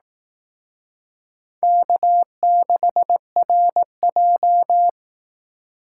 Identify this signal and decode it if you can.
Morse code, heard as a keyed tone, about 18 wpm, 700 Hz. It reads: K6RJ
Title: Speed Racing - USA 1x2 Call Signs 15wpm